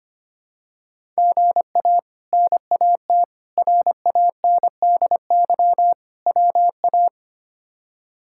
Morse code, heard as a keyed tone, about 25 wpm, 700 Hz. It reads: GA NAT RANDY WA
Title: K1USN SST QSO Practice - 25wpm